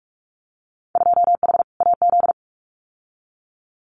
Morse code, heard as a keyed tone, about 46 wpm, 700 Hz. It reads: 2H AZ
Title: Winter Field Day Exchanges 45wpm